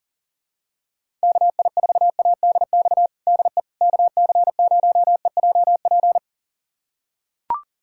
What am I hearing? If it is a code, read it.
Morse code, heard as a keyed tone, about 40 wpm, 700 Hz. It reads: KI4ADX DE KC0EJP
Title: USA Callsign DE Callsign - 6 Characters - 40wpm